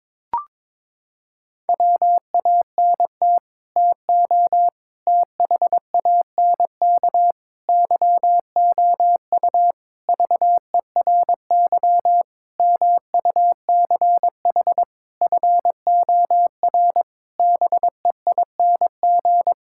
Morse code, heard as a keyed tone, about 22 wpm, 700 Hz. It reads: WANT TO THANK YOU VERY MUCH FOR BEING
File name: Sets of 9 Words - 22wpm